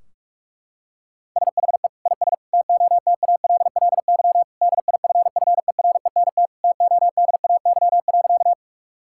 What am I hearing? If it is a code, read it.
Morse code, heard as a keyed tone, about 46 wpm, 700 Hz. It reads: SHE IS TOTALLY DIFFERENT TODAY.
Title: Sentences from Top 1600 Words 45wpm